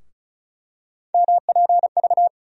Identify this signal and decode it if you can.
Morse code, heard as a keyed tone, about 34 wpm, 700 Hz. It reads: MPV
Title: USA Call Sign Postfixes - 3 Letters 35wpm